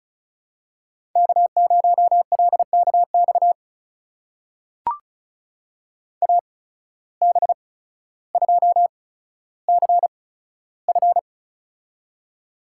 Morse code, heard as a keyed tone, about 36 wpm, spaced as for 15 wpm, 700 Hz. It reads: K0LKX AB2CF
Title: USA Call Signs - 5 Characters - 8x Character Spacing - 35wpm